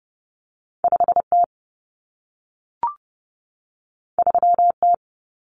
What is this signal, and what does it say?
Morse code, heard as a keyed tone, about 30 wpm, 700 Hz. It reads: 5T 3T
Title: Common Cut Numbers - 2-Digit 30wpm